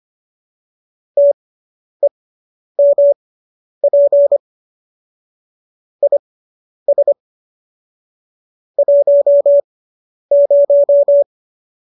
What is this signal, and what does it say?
Morse code, heard as a keyed tone, about 25 wpm, spaced as for 10 wpm, 570 Hz. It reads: TEMP IS 10